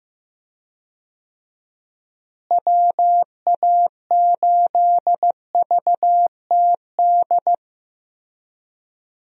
Morse code, heard as a keyed tone, about 14 wpm, 700 Hz. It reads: WA8VTD
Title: USA Call Signs - 6 Characters - Repeated 2x - 15wpm